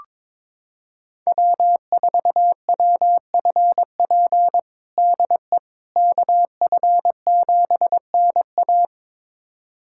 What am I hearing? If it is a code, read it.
Morse code, heard as a keyed tone, about 22 wpm, 700 Hz. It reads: W4WFP DE KF7NA
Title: USA Callsign DE Callsign - 5 Characters - 22wpm